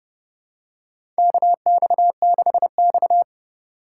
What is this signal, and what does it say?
Morse code, heard as a keyed tone, about 30 wpm, 700 Hz. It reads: KX6X
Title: K1USN SST QSO Practice - 30wpm